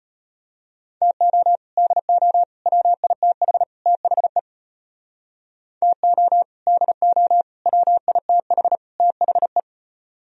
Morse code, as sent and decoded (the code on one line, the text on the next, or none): - --- / -.. --- / .-- .. - .... / - .... / - --- / -.. --- / .-- .. - .... / - .... .
TO DO WITH THE TO DO WITH THE